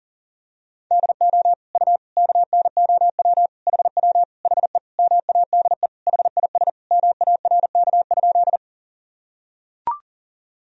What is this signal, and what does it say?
Morse code, heard as a keyed tone, about 37 wpm, 700 Hz. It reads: DO U KNOW HW HE MADE HIS MARK?
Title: Sentences from Top 300 Words and Abbreviations 40wpm